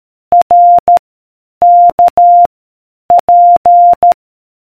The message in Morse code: R K P